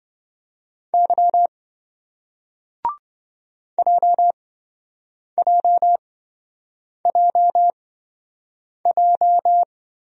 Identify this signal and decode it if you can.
Morse code, heard as a keyed tone, about 24 wpm, spaced as for 7 wpm, 700 Hz. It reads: Y JJJJ
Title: Speed-Racing Single Letters 20wpm